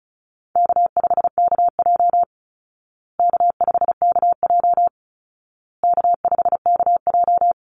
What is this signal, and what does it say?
Morse code, heard as a keyed tone, about 34 wpm, 700 Hz. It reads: K5KJ K5KJ K5KJ
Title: Call Signs Repeated 2x - 35wpm